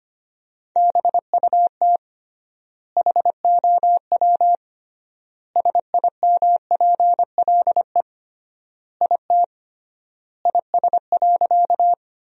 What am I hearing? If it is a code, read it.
Morse code, heard as a keyed tone, about 25 wpm, 700 Hz. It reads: BUT HOW SIMPLE IT IS.
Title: 3x Word-Spacing - Sentences from Top 400 Words - 25wpm